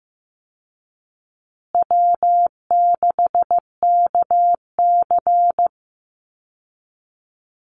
Morse code, heard as a keyed tone, about 15 wpm, 700 Hz. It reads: W6KC
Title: Call Signs Repeated 3x - 15wpm